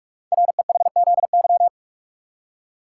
Morse code, heard as a keyed tone, about 44 wpm, 690 Hz. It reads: REHZY
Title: Code Groups - 5 Letters - 45wpm